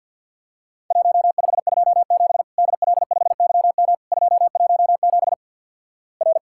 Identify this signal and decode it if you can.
Morse code, heard as a keyed tone, about 50 wpm, 700 Hz. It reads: 1527 DL5YM 217 R
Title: Worked All Europe DX Contest - QTC Exchange with Repeats - 50wpm